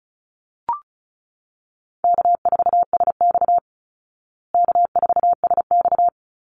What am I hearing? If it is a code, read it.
Morse code, heard as a keyed tone, about 34 wpm, 700 Hz. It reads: K4SX K4SX
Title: Easy Call Signs Repeated 2x - 35wpm